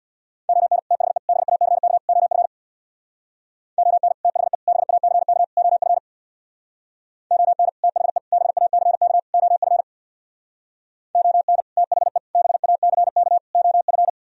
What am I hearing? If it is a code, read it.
Morse code, heard as a keyed tone, about 64 wpm, 700 Hz. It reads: ON THE BACK OF ON THE BACK OF ON THE BACK OF ON THE BACK OF